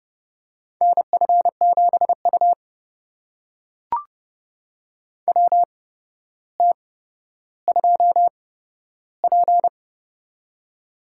Morse code, heard as a keyed tone, about 30 wpm, spaced as for 13 wpm, 700 Hz. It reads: NF7U WT2P
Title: USA Call Signs - 4 Characters - 8x Character Spacing - 30wpm